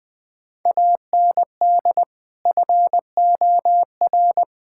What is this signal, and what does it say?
Morse code, heard as a keyed tone, about 21 wpm, 700 Hz. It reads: AND FOR
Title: Build A Phrase - 5-Word Phrases - 20wpm